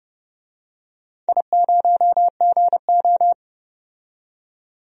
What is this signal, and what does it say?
Morse code, heard as a keyed tone, about 29 wpm, 700 Hz. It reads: I0GO
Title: DXCC-100 Call Signs - 4 Characters - 30wpm